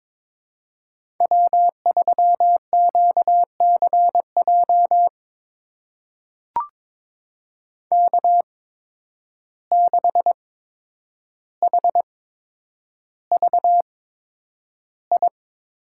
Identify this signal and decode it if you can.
Morse code, heard as a keyed tone, about 22 wpm, spaced as for 9 wpm, 700 Hz. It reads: W3QCJ K6HVI
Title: USA Call Signs - 5 Characters - 8x Character Spacing - 22wpm